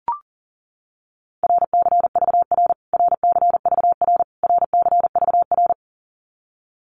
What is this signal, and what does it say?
Morse code, heard as a keyed tone, about 40 wpm, 700 Hz. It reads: RCVR RCVR RCVR
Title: Abbreviations Repeated 3x - 40wpm